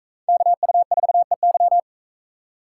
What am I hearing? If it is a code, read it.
Morse code, heard as a keyed tone, about 38 wpm, 700 Hz. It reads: KU4EY